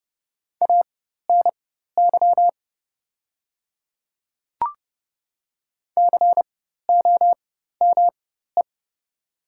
Morse code, heard as a keyed tone, about 30 wpm, spaced as for 14 wpm, 700 Hz. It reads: ANY COME
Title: 4x Character Spacing - Top 300 Words - 30wpm